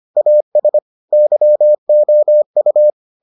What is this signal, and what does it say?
Morse code, heard as a keyed tone, about 25 wpm, 600 Hz. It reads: AS YOU